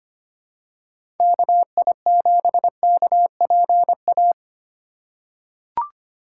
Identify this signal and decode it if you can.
Morse code, heard as a keyed tone, about 25 wpm, 700 Hz. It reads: KI7KPA
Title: USA Call Signs - 6 Characters - 4x Character Spacing - 25wpm